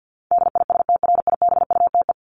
Morse code, heard as a keyed tone, about 48 wpm, 700 Hz. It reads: DISTRIBUTE
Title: Top 1501-1750 Words 50wpm